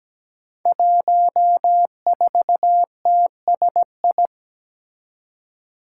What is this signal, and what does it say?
Morse code, heard as a keyed tone, about 17 wpm, 700 Hz. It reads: 14TSI
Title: Five Challenging Letter-Number Combinations 17wpm